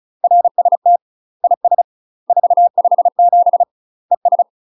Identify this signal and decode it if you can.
Morse code, heard as a keyed tone, about 36 wpm, 700 Hz. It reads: RST IS 457 ES
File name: UK MASTER QSOs TRAINING / 35WPM / UK MASTER QSOs 0009